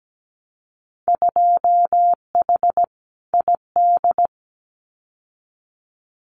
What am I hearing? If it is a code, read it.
Morse code, heard as a keyed tone, about 17 wpm, 700 Hz. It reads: 2H ID
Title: Winter Field Day Exchanges 17wpm